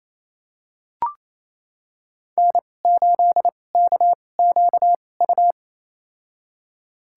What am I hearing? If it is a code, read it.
Morse code, heard as a keyed tone, about 28 wpm, spaced as for 20 wpm, 700 Hz. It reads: N8KQU
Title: USA Call Signs - 5 Characters - 2x Character Spacing - 28wpm